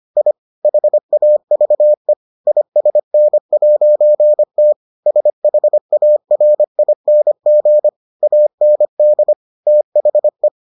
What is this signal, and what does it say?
Morse code, heard as a keyed tone, about 25 wpm, 600 Hz. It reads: I HAVE ISN'T SHARING AND THE